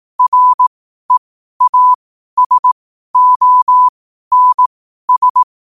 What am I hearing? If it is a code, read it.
Morse code, heard as a keyed tone, about 18 wpm, spaced as for 12 wpm, 1000 Hz. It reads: REASONS
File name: qotdfiles / 12wpm